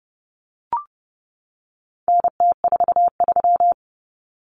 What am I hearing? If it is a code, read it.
Morse code, heard as a keyed tone, about 30 wpm, 700 Hz. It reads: NT43
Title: Common Cut Numbers - 4-Digit 30wpm